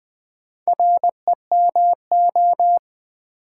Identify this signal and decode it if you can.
Morse code, heard as a keyed tone, about 20 wpm, 700 Hz. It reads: REMO
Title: Build A Word - 2 Letters At A Time - 6-Letter Words - 20wpm